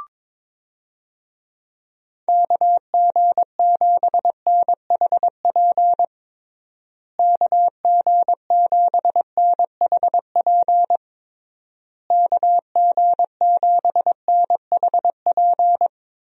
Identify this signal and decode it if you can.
Morse code, heard as a keyed tone, about 21 wpm, 700 Hz. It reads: KG7NHP KG7NHP KG7NHP
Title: USA Call Signs - 6 Characters - Repeated 3x - 22wpm